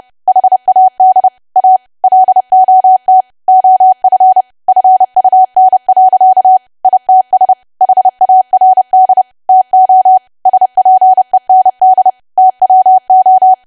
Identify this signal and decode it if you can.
Morse code, heard as a keyed tone, about 30 wpm, 750 Hz. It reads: HAD A LOT OF FUN. ITS HARD TO SPEND TWO
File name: cwa / W1AW-Level3 / 30WPM